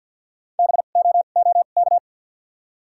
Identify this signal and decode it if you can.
Morse code, heard as a keyed tone, about 53 wpm, 700 Hz. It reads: B O O K
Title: 2x Character Spacing - Top 300 Words - 50wpm